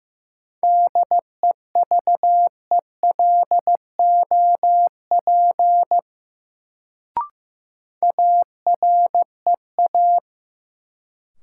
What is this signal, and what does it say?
Morse code, heard as a keyed tone, about 15 wpm, 700 Hz. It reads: DEVELOP AREA
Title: Sentences from Top 1000 Words - Review Part 2 - New Word Drill - 15wpm